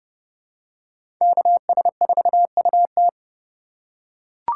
Morse code, heard as a keyed tone, about 30 wpm, 700 Hz. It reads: KS4UT
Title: USA Call Signs - 5 Characters - 2x Character Spacing - 30wpm